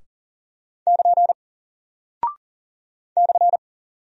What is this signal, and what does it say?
Morse code, heard as a keyed tone, about 40 wpm, 700 Hz. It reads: <KN> /
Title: Single Character (Letter:Number:Punctuation:Prosign) Alt 40wpm